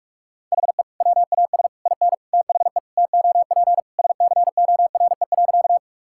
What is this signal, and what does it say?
Morse code, heard as a keyed tone, about 46 wpm, 700 Hz. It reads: HE WAS IN THE TOP SCORE.
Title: Sentences from Top 1600 Words 45wpm